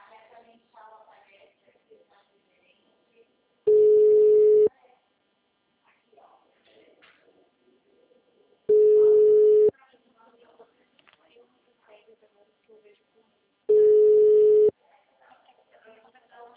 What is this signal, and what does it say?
Ringback tone (Europe, 425 Hz), with speech.